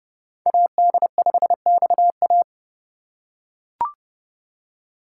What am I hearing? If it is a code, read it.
Morse code, heard as a keyed tone, about 30 wpm, 700 Hz. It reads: AD5XA